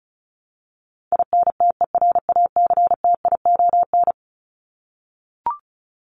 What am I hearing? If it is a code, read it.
Morse code, heard as a keyed tone, about 34 wpm, 700 Hz. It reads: INTERACTION
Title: Top 1251-1500 Words 35wpm